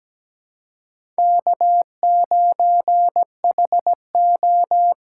Morse code, heard as a keyed tone, about 17 wpm, 700 Hz. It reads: K9HO
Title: USA Call Signs - 5 Characters - 6x Character Spacing - 17wpm